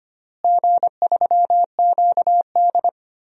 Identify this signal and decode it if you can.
Morse code, heard as a keyed tone, about 25 wpm, 700 Hz. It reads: G3QD
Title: DXCC-100 Call Signs - 4 Characters - Repeated 3x - 25wpm